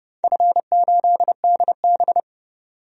Morse code, heard as a keyed tone, about 31 wpm, 700 Hz. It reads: F8DB
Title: DXCC-100 Call Signs - 4 Characters - 30wpm